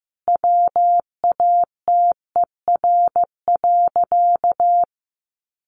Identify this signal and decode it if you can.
Morse code, heard as a keyed tone, about 15 wpm, 700 Hz. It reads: WATER.
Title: A Sound Of Thunder 15wpm